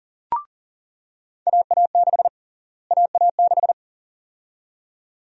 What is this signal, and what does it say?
Morse code, heard as a keyed tone, about 40 wpm, 700 Hz. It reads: AA6 AA6
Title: USA Call Sign Prefixes - Repeated 2x - 40wpm